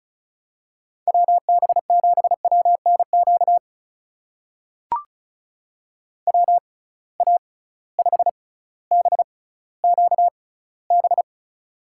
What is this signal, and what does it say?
Morse code, heard as a keyed tone, about 34 wpm, spaced as for 18 wpm, 700 Hz. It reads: WB7WNQ WA5BQB
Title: USA Call Signs - 6 Characters - 6x Character Spacing - 35wpm